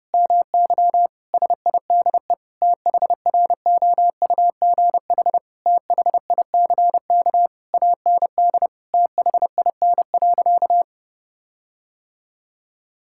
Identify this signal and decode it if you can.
Morse code, heard as a keyed tone, about 32 wpm, 700 Hz. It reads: MY SIDE THROUGH THICK AND THIN.
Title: Sentences from Top 1600 Words 30wpm